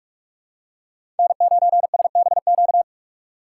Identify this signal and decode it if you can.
Morse code, heard as a keyed tone, about 44 wpm, 700 Hz. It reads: N9SBQ
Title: USA Call Signs - 5 Characters - 6x Character Spacing - 45wpm